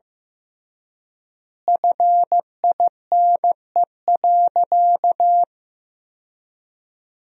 Morse code, heard as a keyed tone, about 15 wpm, 700 Hz. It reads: FINE.
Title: 3x Word-Spacing - Sentences from Top 500 Words - 15wpm